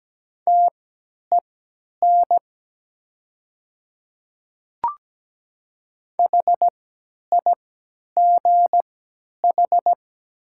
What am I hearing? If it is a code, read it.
Morse code, heard as a keyed tone, about 17 wpm, spaced as for 10 wpm, 700 Hz. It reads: TEN HIGH
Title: 3x Character Spacing - Top 500 Words - 17wpm